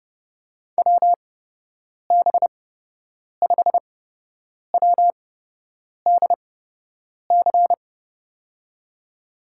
Morse code, heard as a keyed tone, about 30 wpm, spaced as for 8 wpm, 700 Hz. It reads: WB5WDC
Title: USA Call Signs - 6 Characters - 8x Character Spacing - 30wpm